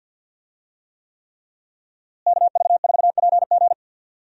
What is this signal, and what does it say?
Morse code, heard as a keyed tone, about 52 wpm, 700 Hz. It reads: KV4PG
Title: USA Call Signs - 5 Characters - Repeated 3x - 50wpm